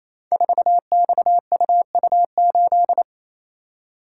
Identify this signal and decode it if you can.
Morse code, heard as a keyed tone, about 27 wpm, 700 Hz. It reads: 4XUU8
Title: Code Groups - 5 Letter:Numbers - 28wpm